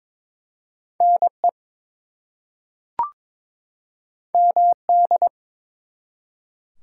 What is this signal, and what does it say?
Morse code, heard as a keyed tone, about 22 wpm, 700 Hz. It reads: NE MD
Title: US States and Canadian Provinces 22wpm